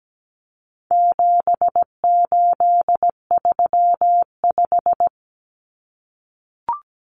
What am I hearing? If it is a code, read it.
Morse code, heard as a keyed tone, about 17 wpm, 700 Hz. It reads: 7835
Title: Numbers - 4-Digit 17wpm